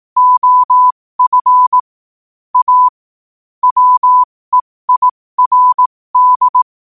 Morse code, heard as a keyed tone, about 18 wpm, spaced as for 15 wpm, 1000 Hz. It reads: OF A WEIRD
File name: qotdfiles / 15wpm